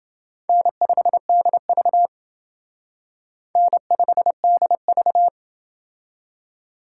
Morse code, heard as a keyed tone, about 28 wpm, 700 Hz. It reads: N5DV N5DV